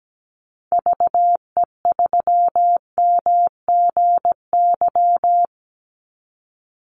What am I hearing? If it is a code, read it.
Morse code, heard as a keyed tone, about 17 wpm, 700 Hz. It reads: VE3MGY
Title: Call Signs Repeated 3x - 17wpm